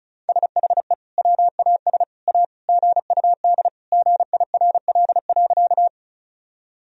Morse code, heard as a keyed tone, about 36 wpm, 700 Hz. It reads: SHE WAS A GUD GIRL.
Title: Sentences from Top 400 Words and Abbreviations 35wpm